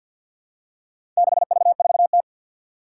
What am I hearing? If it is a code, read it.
Morse code, heard as a keyed tone, about 53 wpm, 700 Hz. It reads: 6V4T
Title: Four Challenging Letter-Number Combinations 50wpm